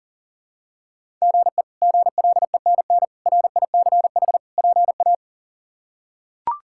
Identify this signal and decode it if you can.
Morse code, heard as a keyed tone, about 37 wpm, 700 Hz. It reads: GE GLENN RICH PA
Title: K1USN SST QSO Practice - 40wpm